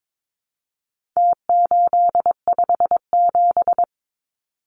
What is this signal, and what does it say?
Morse code, heard as a keyed tone, about 22 wpm, 700 Hz. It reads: T857
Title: Numbers - Contest Style - 4-Digit 22wpm